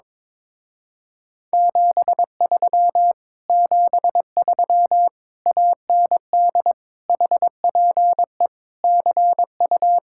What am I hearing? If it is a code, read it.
Morse code, heard as a keyed tone, about 22 wpm, 700 Hz. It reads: 73 73 AND HPE CU
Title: Speed-Racing Common QSO Phrases - 22wpm